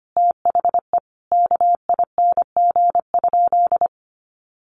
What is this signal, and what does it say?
Morse code, heard as a keyed tone, about 25 wpm, 700 Hz. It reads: THE KING?
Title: Sentences from Top 1400 Words 25wpm